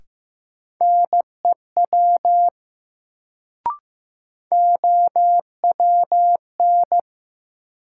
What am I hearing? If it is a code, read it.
Morse code, heard as a keyed tone, about 15 wpm, 700 Hz. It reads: NEW OWN
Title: Sentences from Top 300 Words - New Word Drill - 15wpm